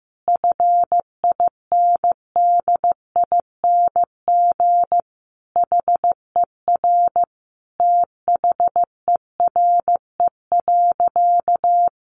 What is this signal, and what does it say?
Morse code, heard as a keyed tone, about 15 wpm, 700 Hz. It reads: FINDING HER THERE.
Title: Sentences from Top 1500 Words 15wpm